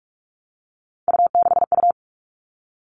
Morse code, heard as a keyed tone, about 44 wpm, 700 Hz. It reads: U6U
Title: All Cut Numbers - 3-Digit 45wpm